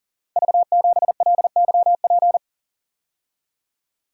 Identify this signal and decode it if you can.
Morse code, heard as a keyed tone, about 38 wpm, 700 Hz. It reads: V7LYP
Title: Code Groups - 5 Letter:Numbers - 40wpm